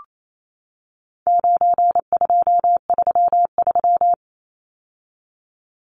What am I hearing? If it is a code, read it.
Morse code, heard as a keyed tone, about 28 wpm, 700 Hz. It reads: 9233
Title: Numbers - 4-Digit 28wpm